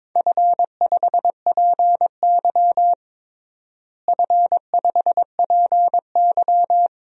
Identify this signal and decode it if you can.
Morse code, heard as a keyed tone, about 22 wpm, 700 Hz. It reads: F5PY F5PY
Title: DXCC-100 Call Signs - 4 Characters - Repeated 3x - 22wpm